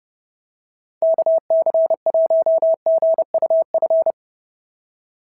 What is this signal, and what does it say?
Morse code, heard as a keyed tone, about 30 wpm, 650 Hz. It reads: KC1GUF
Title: SOTA QSO Practice 30wpm